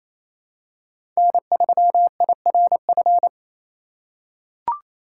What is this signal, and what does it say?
Morse code, heard as a keyed tone, about 28 wpm, 700 Hz. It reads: N3IRF
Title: USA Call Signs - 4x Character Spacing - 28wpm